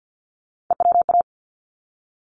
Morse code, heard as a keyed tone, about 50 wpm, 700 Hz. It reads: EWA
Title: ARRL Sections 50wpm